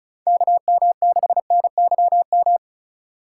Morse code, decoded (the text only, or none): KM6NYM